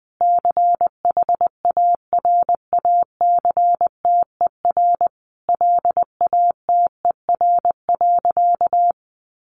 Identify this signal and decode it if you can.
Morse code, heard as a keyed tone, about 19 wpm, 700 Hz. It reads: CHARACTER LATER.
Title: Sentences from Top 1200 Words 20wpm